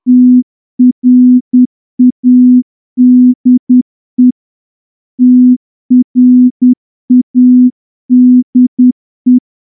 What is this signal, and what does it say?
Morse code, heard as a keyed tone, about 10 wpm, 250 Hz. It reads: TRADE TRADE
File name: morse-training-10WPM